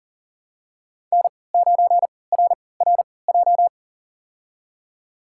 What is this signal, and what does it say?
Morse code, heard as a keyed tone, about 40 wpm, spaced as for 23 wpm, 700 Hz. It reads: N9RRJ
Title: USA Call Signs - 3x Character Spacing - 40wpm